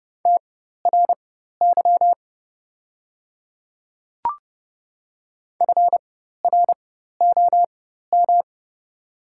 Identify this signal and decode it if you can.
Morse code, heard as a keyed tone, about 30 wpm, spaced as for 14 wpm, 700 Hz. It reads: TRY FROM
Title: 4x Character Spacing - Top 300 Words - 30wpm